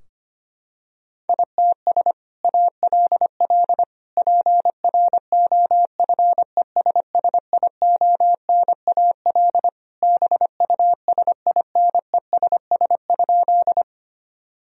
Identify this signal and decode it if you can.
Morse code, heard as a keyed tone, about 25 wpm, 700 Hz. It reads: ITS ALL PROFESSIONAL BUSINESS?
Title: Sentences from Top 1000 Words - Review Part 4 - 25wpm